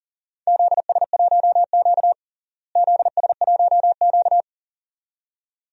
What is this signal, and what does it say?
Morse code, heard as a keyed tone, about 40 wpm, 700 Hz. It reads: ZS1Q ZS1Q
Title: DXCC-100 Call Signs - 4 Characters - Repeated 2x - 40wpm